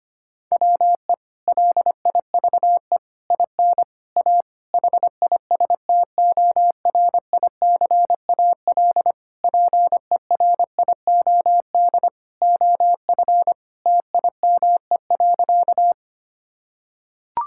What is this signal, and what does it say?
Morse code, heard as a keyed tone, about 25 wpm, 700 Hz. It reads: WE LIVE IN A HISTORICAL PERIOD OF TIME.
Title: Sentences from Top 1000 Words - Review Part 5 - 25wpm